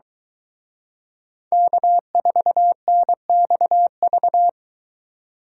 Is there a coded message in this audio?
Yes: Morse code from 1.5 to 4.5 s, heard as a keyed tone, about 22 wpm, 700 Hz. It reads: K4NXV